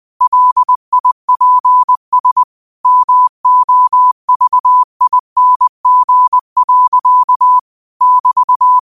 Morse code, heard as a keyed tone, about 20 wpm, 1000 Hz. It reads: LIPS MOVING. <BT>
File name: qotdfiles / 20wpm